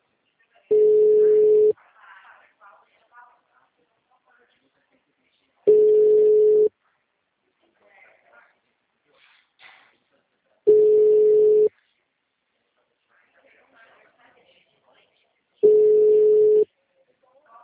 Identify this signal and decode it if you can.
Ringback tone (Europe, 425 Hz), with speech.